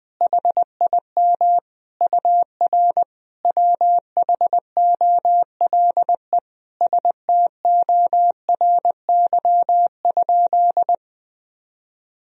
Morse code, heard as a keyed tone, about 21 wpm, 700 Hz. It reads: HIM UR WHOLE STORY?
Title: Sentences from Top 400 Words and Abbreviations 20wpm